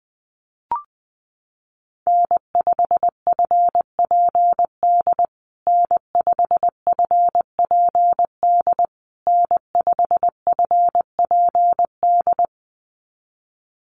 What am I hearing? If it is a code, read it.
Morse code, heard as a keyed tone, about 20 wpm, 700 Hz. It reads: N5FPD N5FPD N5FPD
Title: USA Call Signs Repeated 3x 20wpm